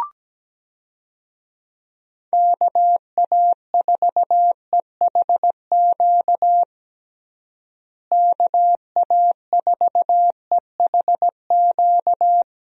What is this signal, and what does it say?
Morse code, heard as a keyed tone, about 17 wpm, 700 Hz. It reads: KA4EHQ KA4EHQ
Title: USA Call Signs - 6 Characters - Repeated 3x - 17wpm